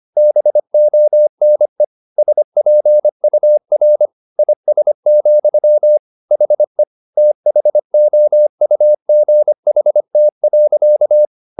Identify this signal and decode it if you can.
Morse code, heard as a keyed tone, about 25 wpm, 600 Hz. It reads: BONE SPUR IS, HE THOUGHT.